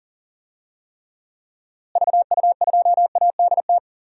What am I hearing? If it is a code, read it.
Morse code, heard as a keyed tone, about 40 wpm, 700 Hz. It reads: VU2ADT
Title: DXCC-100 Call Signs - 6 Characters - Repeated 2x - 40wpm